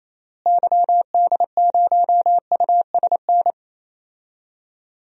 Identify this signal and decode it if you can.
Morse code, heard as a keyed tone, about 28 wpm, 700 Hz. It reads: YD0USN
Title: DXCC-100 Call Signs - 6 Characters - Repeated 3x - 28wpm